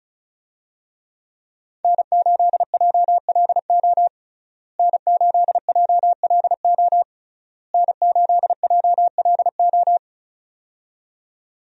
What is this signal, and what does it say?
Morse code, heard as a keyed tone, about 34 wpm, 700 Hz. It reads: N8JLO N8JLO N8JLO
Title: USA Call Signs - 5 Characters - Repeated 3x - 35wpm